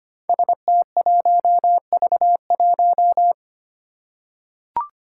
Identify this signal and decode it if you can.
Morse code, heard as a keyed tone, about 25 wpm, 700 Hz. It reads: ST1V1
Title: Five Challenging Letter-Number Combinations 25wpm